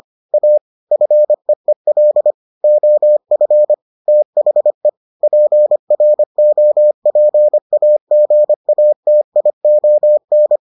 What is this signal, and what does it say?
Morse code, heard as a keyed tone, about 25 wpm, 600 Hz. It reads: A FEEL OF THE PROPAGATION